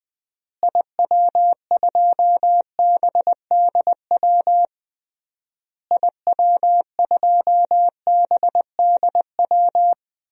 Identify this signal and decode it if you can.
Morse code, heard as a keyed tone, about 20 wpm, 700 Hz. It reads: IW2BDW IW2BDW
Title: DXCC-100 Call Signs - 6 Characters - Repeated 3x - 20wpm